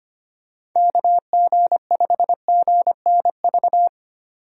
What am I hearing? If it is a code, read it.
Morse code, heard as a keyed tone, about 25 wpm, 700 Hz. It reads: KG5GNV
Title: USA Call Signs - 6 Characters - 25wpm